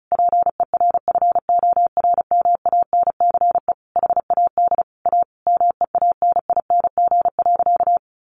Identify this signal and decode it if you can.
Morse code, heard as a keyed tone, about 34 wpm, 700 Hz. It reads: PERFORMANCE HAD A MEANING.
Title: Sentences from Top 1400 Words 35wpm